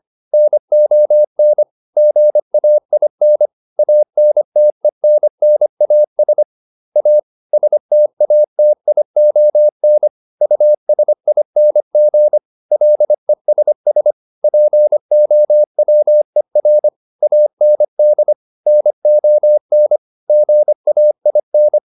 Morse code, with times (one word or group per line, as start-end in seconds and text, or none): 0.3-1.6 NON
2.0-3.5 GAIN
3.8-6.4 ANTENNAS
7.0-7.2 A
7.5-10.1 STATION
10.4-12.4 USING
12.7-14.1 LESS
14.4-16.9 POWER
17.2-18.3 AND
18.7-20.0 NON
20.3-21.8 GAIN